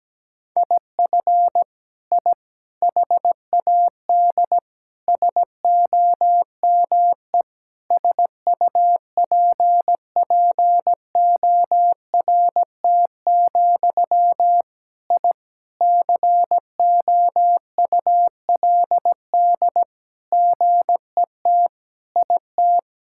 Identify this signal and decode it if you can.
Morse code, heard as a keyed tone, about 17 wpm, 700 Hz. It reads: IF I HAD SOME SUPPORT, I COULD GET IT
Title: Sentences from Top 1000 Words - Review Part 5 - 17wpm